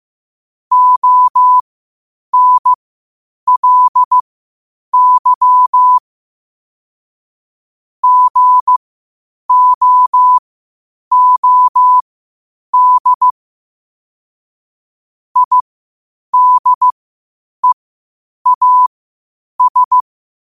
Morse code, heard as a keyed tone, about 15 wpm, spaced as for 8 wpm, 1000 Hz. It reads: ONLY GOOD IDEAS
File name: qotdfiles / 8wpm